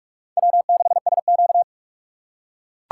Morse code, heard as a keyed tone, about 43 wpm, 700 Hz. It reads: W6SQ
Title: USA Call Signs - 4 Characters - 4x Character Spacing - 45wpm